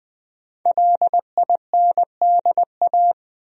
Morse code, heard as a keyed tone, about 20 wpm, 700 Hz. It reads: LINDA
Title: USA Common Names - 20wpm